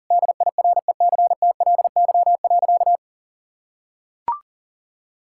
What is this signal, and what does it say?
Morse code, heard as a keyed tone, about 37 wpm, 700 Hz. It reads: DIRECTLY.
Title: Sentences from Top 1000 Words - Review Part 5 - 40wpm